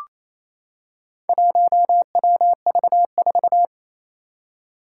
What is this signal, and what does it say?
Morse code, heard as a keyed tone, about 29 wpm, 700 Hz. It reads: 1WV4
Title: Four Challenging Letter-Number Combinations 28wpm